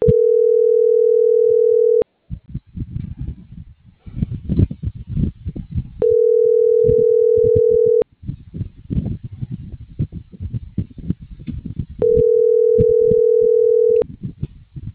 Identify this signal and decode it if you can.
Ringback tone (North America), 440 + 480 Hz.